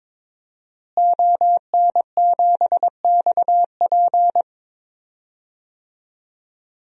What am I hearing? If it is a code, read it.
Morse code, heard as a keyed tone, about 21 wpm, 700 Hz. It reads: ON7XP
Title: DXCC-100 Call Signs - 5 Characters - Repeated 2x - 22wpm